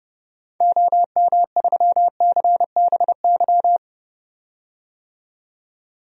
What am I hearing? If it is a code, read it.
Morse code, heard as a keyed tone, about 30 wpm, 700 Hz. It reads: OM3CBY
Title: DXCC-100 Call Signs - 6 Characters - 30wpm